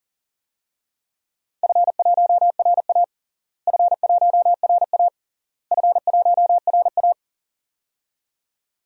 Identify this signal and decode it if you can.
Morse code, heard as a keyed tone, about 40 wpm, 700 Hz. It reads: F1RA F1RA F1RA
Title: DXCC-100 Call Signs - 4 Characters - Repeated 3x - 40wpm